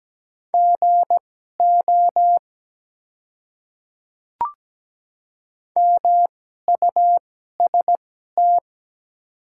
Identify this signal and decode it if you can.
Morse code, heard as a keyed tone, about 17 wpm, spaced as for 12 wpm, 690 Hz. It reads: GO MUST
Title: 2x Character Spacing - Top 300 Words - 17wpm